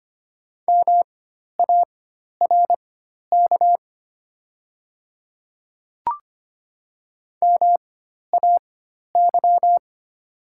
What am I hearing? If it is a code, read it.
Morse code, heard as a keyed tone, about 25 wpm, spaced as for 12 wpm, 700 Hz. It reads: MARK MAY
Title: 4x Character Spacing - Top 300 Words - 25wpm